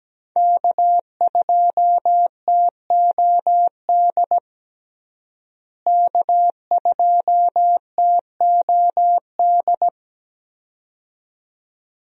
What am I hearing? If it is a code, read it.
Morse code, heard as a keyed tone, about 17 wpm, 700 Hz. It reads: K2TOD K2TOD
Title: USA Call Signs - 5 Characters - Repeated 2x - 17wpm